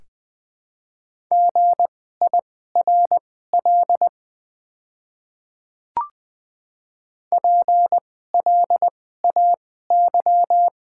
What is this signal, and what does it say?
Morse code, heard as a keyed tone, about 21 wpm, spaced as for 15 wpm, 700 Hz. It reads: GIRL PLAY
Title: 2x Character Spacing - Top 500 Words - 20wpm